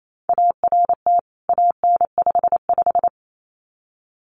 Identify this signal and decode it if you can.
Morse code, heard as a keyed tone, about 28 wpm, 700 Hz. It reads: ART AN55
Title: CWT Exchange 28wpm